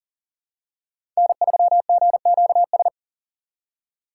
Morse code, heard as a keyed tone, about 40 wpm, 700 Hz. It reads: N3GQS
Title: USA Call Signs - 5 Characters - 4x Character Spacing - 40wpm